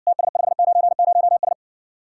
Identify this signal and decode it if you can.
Morse code, heard as a keyed tone, about 60 wpm, 700 Hz. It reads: TS590S